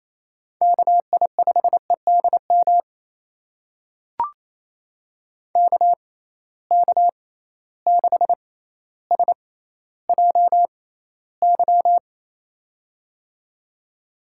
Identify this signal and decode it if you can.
Morse code, heard as a keyed tone, about 28 wpm, spaced as for 14 wpm, 700 Hz. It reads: KI5EDM KK6SJY
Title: USA Call Signs - 6 Characters - 6x Character Spacing - 28wpm